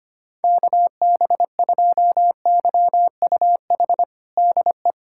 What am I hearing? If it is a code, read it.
Morse code, heard as a keyed tone, about 25 wpm, 700 Hz. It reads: KB2YUH DE
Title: USA Callsign DE Callsign - 6 Characters - 25wpm